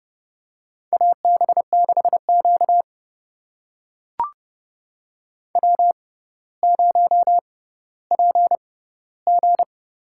Morse code, heard as a keyed tone, about 30 wpm, spaced as for 15 wpm, 700 Hz. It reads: AB6Q W0PG
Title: USA Call Signs - 4 Characters - 6x Character Spacing - 30wpm